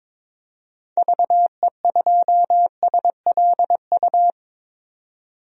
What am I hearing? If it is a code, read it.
Morse code, heard as a keyed tone, about 22 wpm, 700 Hz. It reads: VE2SLU
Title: DXCC-100 Call Signs - 6 Characters - 22wpm